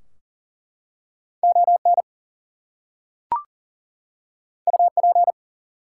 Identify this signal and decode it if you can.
Morse code, heard as a keyed tone, about 40 wpm, 700 Hz. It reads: ON UP
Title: Two Letter Words - 40wpm